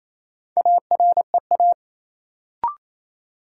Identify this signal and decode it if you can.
Morse code, heard as a keyed tone, about 28 wpm, 700 Hz. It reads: AREA
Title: Sentences from Top 1000 Words - Review Part 2 - New Word Drill - 28wpm